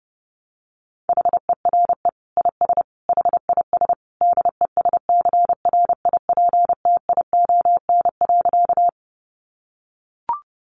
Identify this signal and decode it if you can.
Morse code, heard as a keyed tone, about 30 wpm, 700 Hz. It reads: HERE IS HIS DESCRIPTION.
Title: Sentences from Top 1300 Words 30wpm